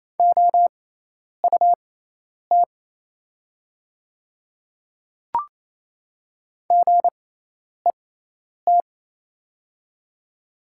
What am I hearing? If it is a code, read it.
Morse code, heard as a keyed tone, about 28 wpm, spaced as for 10 wpm, 690 Hz. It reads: OUT GET
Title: 6x Character Spacing - Top 500 Words - 28wpm